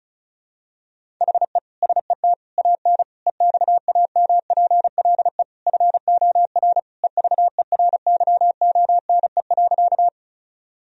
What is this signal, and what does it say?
Morse code, heard as a keyed tone, about 34 wpm, 700 Hz. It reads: HE SET AN EXAMPLE FOR EVERYONE.
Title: Sentences from Top 1000 Words - Review Part 1 - 35wpm